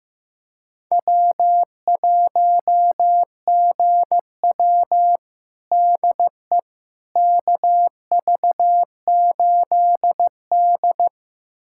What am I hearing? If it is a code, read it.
Morse code, heard as a keyed tone, about 15 wpm, 700 Hz. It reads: W1GW DE KV8D
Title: USA Callsign DE Callsign - 4 Characters - 15wpm